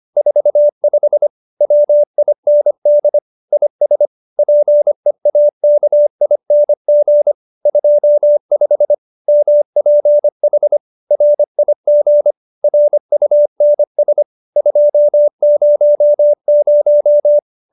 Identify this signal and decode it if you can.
Morse code, heard as a keyed tone, about 25 wpm, 590 Hz. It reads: 45 WIND IS PEAKING 25 MPH RIG RUNS 200
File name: QSO306 25